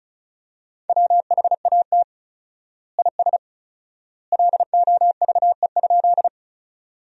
Morse code, heard as a keyed tone, about 36 wpm, 700 Hz. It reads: WHAT IS LOVE?
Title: 4x Word-Spacing - Sentences from Top 400 Words - 35wpm